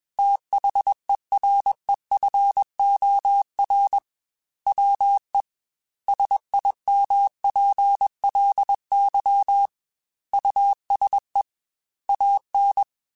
Morse code, heard as a keyed tone, about 21 wpm, 800 Hz. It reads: THEREFOR WE SIMPLY USE AN
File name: Technical Subjects / Technical Subjects-mp3s / Sw Pwr Sup @20